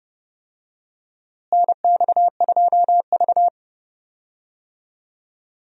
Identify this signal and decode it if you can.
Morse code, heard as a keyed tone, about 30 wpm, 700 Hz. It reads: NX2V
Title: USA Call Signs - 4 Characters - Repeated 3x - 30wpm